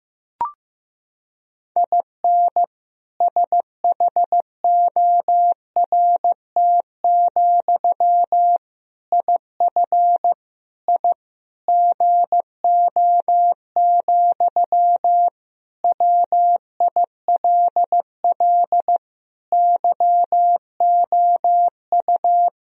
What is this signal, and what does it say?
Morse code, heard as a keyed tone, about 15 wpm, 700 Hz. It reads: IN SHORT, IF I GO, WILL YOU
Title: Sentences from Top 1000 Words - Review Part 2 - 15wpm